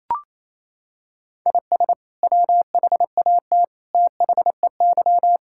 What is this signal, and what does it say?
Morse code, heard as a keyed tone, about 28 wpm, 700 Hz. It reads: IS WHAT THEY
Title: Sentences from Top 1000 Words - Review Part 5 - 28wpm